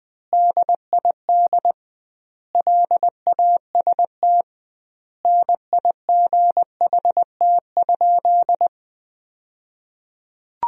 Morse code, heard as a keyed tone, about 21 wpm, 700 Hz. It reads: DID LAST NIGHT?
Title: 2x Word-Spacing - Sentences from Top 300 Words - 20wpm